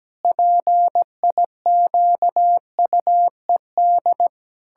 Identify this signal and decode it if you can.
Morse code, heard as a keyed tone, about 17 wpm, 700 Hz. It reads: PIQUED
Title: Pangram Challenge v2 - 17wpm